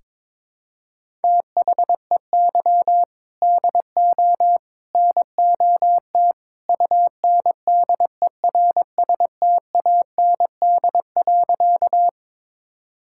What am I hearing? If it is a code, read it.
Morse code, heard as a keyed tone, about 22 wpm, 700 Hz. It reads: THEY DO NOT UNDERSTAND.
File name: practice / 22wpm / Sentences with 3-Word Phrases - 22wpm